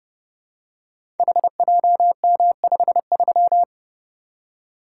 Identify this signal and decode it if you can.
Morse code, heard as a keyed tone, about 28 wpm, 700 Hz. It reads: HJM53
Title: Code Groups - 5 Letter:Numbers - 30wpm